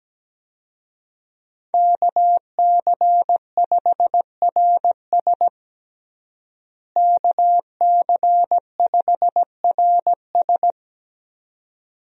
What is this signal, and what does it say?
Morse code, heard as a keyed tone, about 17 wpm, 700 Hz. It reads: KC5RS KC5RS
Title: USA Call Signs - 5 Characters - Repeated 2x - 17wpm